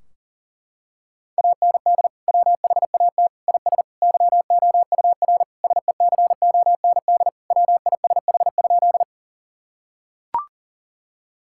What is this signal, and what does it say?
Morse code, heard as a keyed tone, about 40 wpm, 700 Hz. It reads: AND WHAT IS YOUR SECOND WISH?
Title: Sentences from Top 1600 Words 40wpm